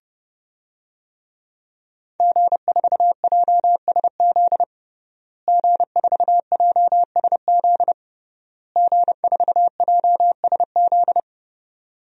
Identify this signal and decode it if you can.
Morse code, heard as a keyed tone, about 31 wpm, 700 Hz. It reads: G4JSZ G4JSZ G4JSZ
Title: DXCC-100 Call Signs - 5 Characters - Repeated 3x - 30wpm